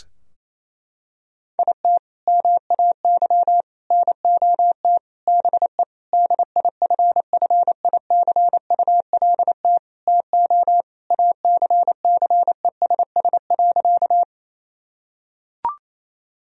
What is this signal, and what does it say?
Morse code, heard as a keyed tone, about 28 wpm, 700 Hz. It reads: IT MAY NOT BE DIFFICULT TO ACCESS.
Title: Sentences from Top 1000 Words - Review Part 4 - 28wpm